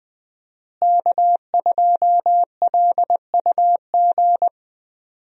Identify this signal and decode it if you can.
Morse code, heard as a keyed tone, about 20 wpm, 700 Hz. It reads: K2LUG
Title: USA Call Signs - 5 Characters - Repeated 3x - 20wpm